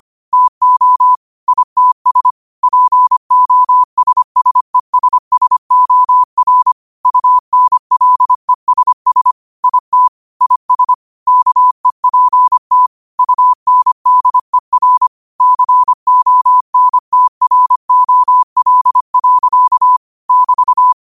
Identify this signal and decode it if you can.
Morse code, heard as a keyed tone, about 25 wpm, 1000 Hz. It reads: TO ITS POSSESSOR UNLESS IT IS KEPT UNDER CONTROL. <BT>
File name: qotdfiles / 25wpm